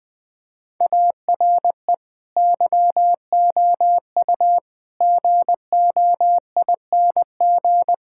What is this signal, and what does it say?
Morse code, heard as a keyed tone, about 20 wpm, 700 Hz. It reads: ARE YOU GOING
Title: Sentences with 5-Word Phrases - 20wpm